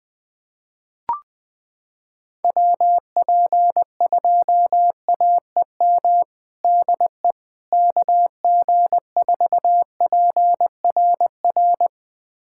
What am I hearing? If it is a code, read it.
Morse code, heard as a keyed tone, about 21 wpm, 700 Hz. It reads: WP2AEM DE KG4PRR
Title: USA Callsign DE Callsign - 6 Characters - 20wpm